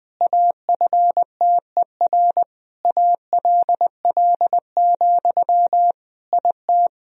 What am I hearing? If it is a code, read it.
Morse code, heard as a keyed tone, about 20 wpm, 700 Hz. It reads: AFTER ALL, IT
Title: Sentences from Top 500 Words and Abbreviations 20wpm